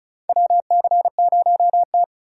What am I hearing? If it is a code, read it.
Morse code, heard as a keyed tone, about 34 wpm, 700 Hz. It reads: WC0T
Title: USA Call Signs - 4 Characters - 3x Character Spacing - 35wpm